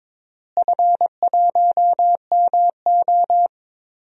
Morse code, heard as a keyed tone, about 22 wpm, 700 Hz. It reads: F1MO
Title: DXCC-100 Call Signs - 4 Characters - Repeated 3x - 22wpm